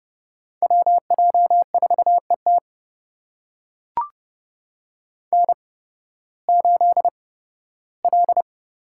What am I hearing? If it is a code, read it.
Morse code, heard as a keyed tone, about 30 wpm, spaced as for 16 wpm, 700 Hz. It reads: WJ4ET N8L
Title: USA Call Signs - 5 Characters - 8x Character Spacing - 30wpm